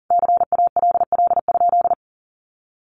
Morse code, heard as a keyed tone, about 40 wpm, 700 Hz. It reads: CALL?
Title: Sentences from Top 100 Words 40wpm